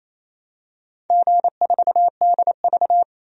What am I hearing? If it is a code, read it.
Morse code, heard as a keyed tone, about 28 wpm, 700 Hz. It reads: G4DV
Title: DXCC-100 Call Signs - 4 Characters - 28wpm